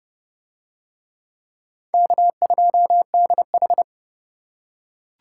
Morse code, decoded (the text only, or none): K2DH